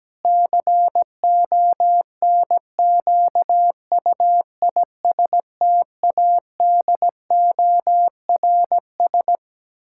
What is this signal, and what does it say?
Morse code, heard as a keyed tone, about 17 wpm, 700 Hz. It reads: CONQUISTADORS
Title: Pangram Challenge v2 - 17wpm